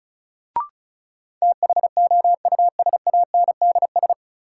Morse code, heard as a keyed tone, about 36 wpm, 700 Hz. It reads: THOUSANDS
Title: Sets of 4 Words - New Word Drill - 35wpm